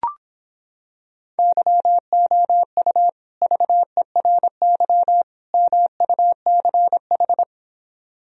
Morse code, heard as a keyed tone, about 26 wpm, 700 Hz. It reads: YOU VERY MUCH